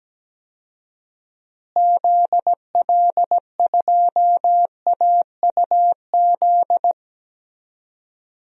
Morse code, heard as a keyed tone, about 17 wpm, 700 Hz. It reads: ZL2AUZ
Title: DXCC-100 Call Signs - 6 Characters - Repeated 3x - 17wpm